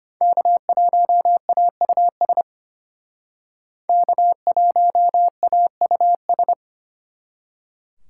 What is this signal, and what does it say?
Morse code, heard as a keyed tone, about 32 wpm, 700 Hz. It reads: K1AUS K1AUS